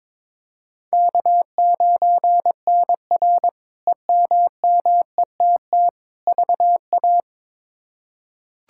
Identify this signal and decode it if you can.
Morse code, heard as a keyed tone, about 22 wpm, 700 Hz. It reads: K9NR EMMETT VA
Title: K1USN SST QSO Practice - 22wpm